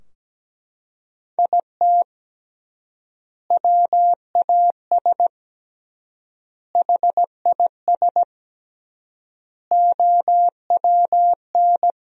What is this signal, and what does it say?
Morse code, heard as a keyed tone, about 17 wpm, 700 Hz. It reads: IT WAS HIS OWN
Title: 3x Word-Spacing - Sentences from Top 300 Words - 17wpm